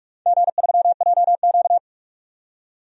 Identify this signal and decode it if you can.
Morse code, heard as a keyed tone, about 44 wpm, 700 Hz. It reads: G3JQ
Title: DXCC-100 Call Signs - 4 Characters - 45wpm